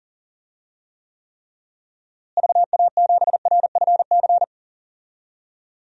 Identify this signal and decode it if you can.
Morse code, heard as a keyed tone, about 40 wpm, 700 Hz. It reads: VA7RFC
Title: DXCC-100 Call Signs - 6 Characters - Repeated 3x - 40wpm